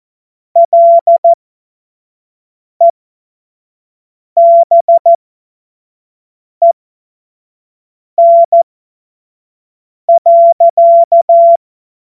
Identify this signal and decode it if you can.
Morse code, heard as a keyed tone, about 14 wpm, spaced as for 5 wpm, 680 Hz. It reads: LEBEN.
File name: Morsekurs / Witze / Witz 08. November 2022 25 F70